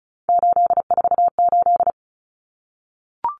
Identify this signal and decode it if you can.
Morse code, heard as a keyed tone, about 34 wpm, 700 Hz. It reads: 848
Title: Common Cut Numbers - 3-Digit 35wpm